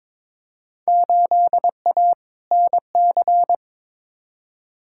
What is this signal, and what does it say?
Morse code, heard as a keyed tone, about 22 wpm, 700 Hz. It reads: 8A NC
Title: Field Day Exchanges - 22wpm